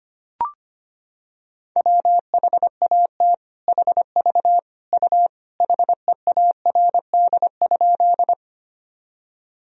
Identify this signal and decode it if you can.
Morse code, heard as a keyed tone, about 25 wpm, 700 Hz. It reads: WHAT HV U HEARD?
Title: Sentences from Top 400 Words and Abbreviations 25wpm